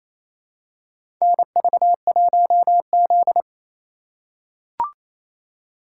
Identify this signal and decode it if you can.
Morse code, heard as a keyed tone, about 28 wpm, 700 Hz. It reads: NV1Z
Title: USA Call Signs - 8x Character Spacing - 28wpm